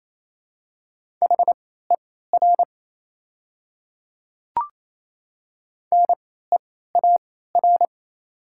Morse code, heard as a keyed tone, about 28 wpm, spaced as for 16 wpm, 700 Hz. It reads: HER NEAR
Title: 3x Character Spacing - Top 200 Words - 28wpm